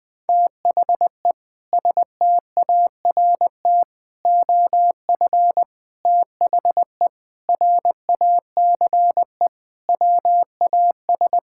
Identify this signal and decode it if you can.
Morse code, heard as a keyed tone, about 20 wpm, 700 Hz. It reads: THE START OF THE RACE WAS
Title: Sentences with 4-Word Phrases - 20wpm